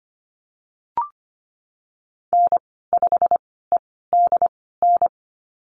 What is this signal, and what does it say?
Morse code, heard as a keyed tone, about 25 wpm, spaced as for 16 wpm, 700 Hz. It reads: N5EDN
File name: Lesson 15g - Number 5 - Call Signs 25wpm